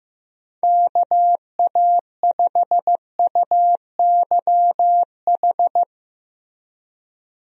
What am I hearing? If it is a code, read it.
Morse code, heard as a keyed tone, about 15 wpm, 700 Hz. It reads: KA5UYH
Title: USA Call Signs - 6 Characters - Repeated 3x - 15wpm